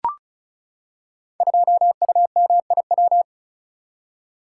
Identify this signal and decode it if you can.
Morse code, heard as a keyed tone, about 36 wpm, 700 Hz. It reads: 2UMIW
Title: Code Groups - 5 Letter:Numbers - 35wpm